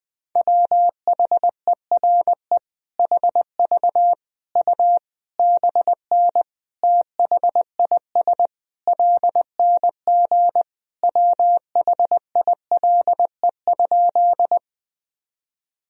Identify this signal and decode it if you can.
Morse code, heard as a keyed tone, about 20 wpm, 700 Hz. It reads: WHERE HV U BN THIS LNG WHILE?
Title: Sentences from Top 300 Words and Abbreviations 20wpm